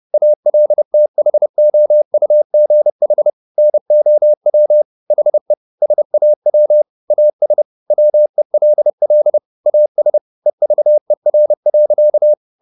Morse code, heard as a keyed tone, about 30 wpm, 600 Hz. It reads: ALTHOUGH NOW HE SAW AS WELL AS EVER.